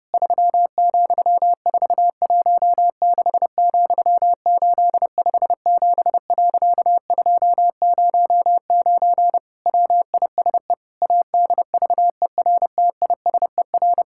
Morse code, heard as a keyed tone, about 30 wpm, 700 Hz. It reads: 3,416,857.209 WISE ADVERTISER
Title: Pangram Challenge v2 - 30wpm